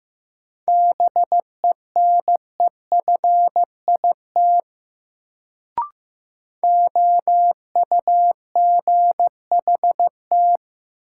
Morse code, heard as a keyed tone, about 15 wpm, 700 Hz. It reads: BENEFIT OUGHT
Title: Sets of 5 Words Encore - New Word Drill - 15wpm